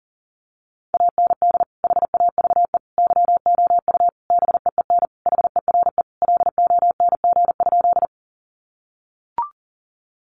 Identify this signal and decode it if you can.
Morse code, heard as a keyed tone, about 40 wpm, 700 Hz. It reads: AND HAVE YOU BEEN HERE LONG?
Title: Sentences from Top 200 Words 40wpm